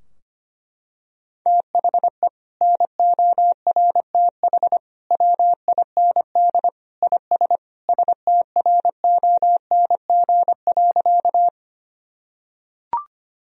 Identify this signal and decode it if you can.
Morse code, heard as a keyed tone, about 25 wpm, 700 Hz. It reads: THE NORTH WIND IS STRONG.
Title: Sentences from Top 1000 Words - Review Part 2 - 25wpm